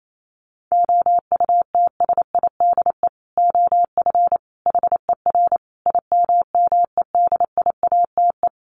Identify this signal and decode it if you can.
Morse code, heard as a keyed tone, about 28 wpm, 700 Hz. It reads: OUTSIDE OF HER IMMEDIATE
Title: Sentences from Top 1200 Words 28wpm